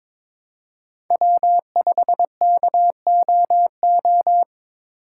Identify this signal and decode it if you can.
Morse code, heard as a keyed tone, about 22 wpm, 700 Hz. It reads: W5KOO
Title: USA Call Signs - 6x Character Spacing - 22wpm